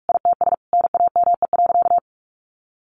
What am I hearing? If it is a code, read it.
Morse code, heard as a keyed tone, about 44 wpm, 700 Hz. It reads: ITS NAME.
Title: Sentences from Top 1500 Words 45wpm